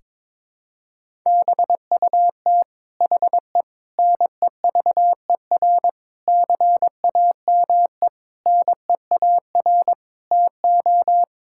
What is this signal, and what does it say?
Morse code, heard as a keyed tone, about 22 wpm, 700 Hz. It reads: BUT HE NEVER CAME NEAR TO
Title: Sentences from Top 1600 Words 22wpm